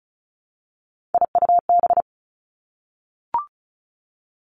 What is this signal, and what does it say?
Morse code, heard as a keyed tone, about 36 wpm, 700 Hz. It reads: IUB
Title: Three Challenging Letter-Number Combinations 35wpm